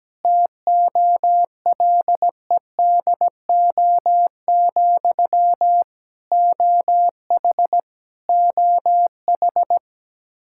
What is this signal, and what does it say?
Morse code, heard as a keyed tone, about 17 wpm, 700 Hz. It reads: TOLEDO, OH OH
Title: Practice QSO - Exchange 1 - Beginning - 17wpm